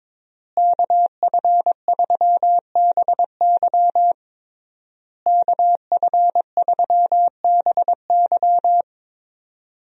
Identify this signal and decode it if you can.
Morse code, heard as a keyed tone, about 22 wpm, 700 Hz. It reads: KF3BY KF3BY
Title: USA Call Signs - 5 Characters - Repeated 3x - 22wpm